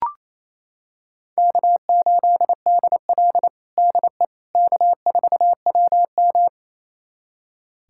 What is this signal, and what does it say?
Morse code, heard as a keyed tone, about 28 wpm, 700 Hz. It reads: K8DL DE K4WM
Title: USA Callsign DE Callsign - 4 Characters - 28wpm